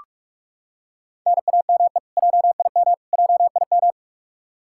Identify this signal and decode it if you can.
Morse code, heard as a keyed tone, about 46 wpm, 700 Hz. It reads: NAME JIM JIM
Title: Practice QSO - Exchange 1 - Beginning - 45wpm